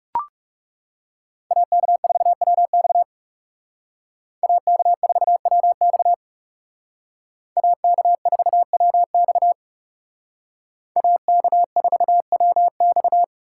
Morse code, heard as a keyed tone, about 38 wpm, 700 Hz. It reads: AK4WX AK4WX AK4WX AK4WX